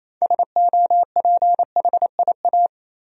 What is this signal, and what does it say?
Morse code, heard as a keyed tone, about 28 wpm, 700 Hz. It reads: SOPHIA
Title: USA Common Names - 28wpm